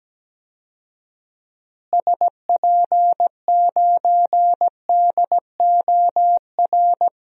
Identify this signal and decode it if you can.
Morse code, heard as a keyed tone, about 17 wpm, 700 Hz. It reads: SP9DOR
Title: DXCC-100 Call Signs - 6 Characters - Repeated 3x - 17wpm